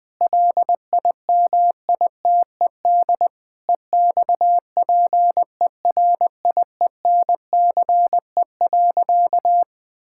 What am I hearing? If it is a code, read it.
Morse code, heard as a keyed tone, about 20 wpm, 700 Hz. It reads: LIMITED EXPERIENCE.
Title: Sentences from Top 1000 Words - Review Part 6 - 20wpm